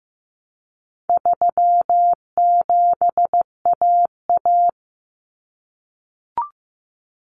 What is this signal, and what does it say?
Morse code, heard as a keyed tone, about 15 wpm, 700 Hz. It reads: 37AA
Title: Common Cut Numbers - 4-Digit 15wpm